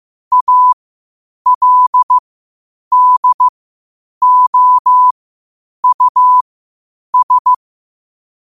Morse code, heard as a keyed tone, about 15 wpm, spaced as for 8 wpm, 1000 Hz. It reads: ALDOUS
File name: qotdfiles / 8wpm